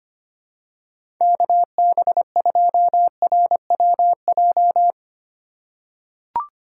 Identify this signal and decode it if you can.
Morse code, heard as a keyed tone, about 25 wpm, 700 Hz. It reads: KB2RWJ
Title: USA Call Signs - 6 Characters - 3x Character Spacing - 25wpm